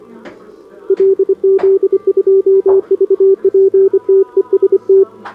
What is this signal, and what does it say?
Morse code, heard as a keyed tone, about 26 wpm, 390 Hz, with radio noise. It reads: LZ2VP TEST
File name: CQWWCW2025 / SobotaOdpol 20m Hudba